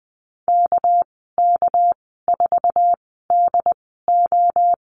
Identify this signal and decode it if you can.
Morse code, heard as a keyed tone, about 19 wpm, spaced as for 14 wpm, 700 Hz. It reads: KK4DO
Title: 200 Lesson 32g - Letter k - Call Signs 20wpm